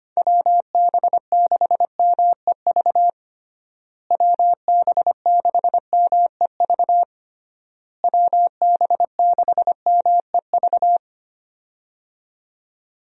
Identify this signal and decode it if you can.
Morse code, heard as a keyed tone, about 25 wpm, 700 Hz. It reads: WB6MEV WB6MEV WB6MEV
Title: USA Call Signs - 6 Characters - Repeated 3x - 25wpm